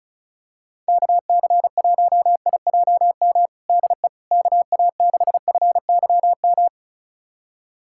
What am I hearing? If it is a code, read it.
Morse code, heard as a keyed tone, about 36 wpm, 700 Hz. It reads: KC1IJM DE KA6FYM
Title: USA Callsign DE Callsign - 6 Characters - 35wpm